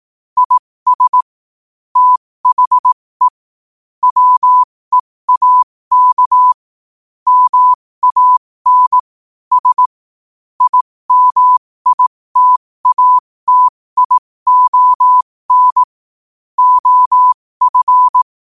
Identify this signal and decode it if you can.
Morse code, heard as a keyed tone, about 18 wpm, spaced as for 15 wpm, 1000 Hz. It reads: IS THE WEAK MAN S IMITATION OF
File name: qotdfiles / 15wpm